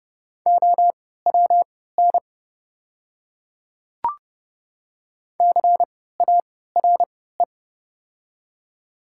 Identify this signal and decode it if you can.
Morse code, heard as a keyed tone, about 30 wpm, spaced as for 17 wpm, 690 Hz. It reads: OWN CARE
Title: 3x Character Spacing - Top 500 Words - 30wpm